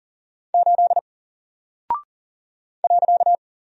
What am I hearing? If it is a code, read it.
Morse code, heard as a keyed tone, about 40 wpm, 700 Hz. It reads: 8 .
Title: Single Character (Letter:Number:Punctuation:Prosign) 40wpm